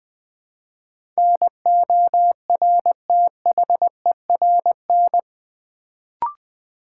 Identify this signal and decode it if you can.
Morse code, heard as a keyed tone, about 20 wpm, 700 Hz. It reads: NORTHERN
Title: Build A Word - 2 Letters At A Time - 8-Letter Words - 20wpm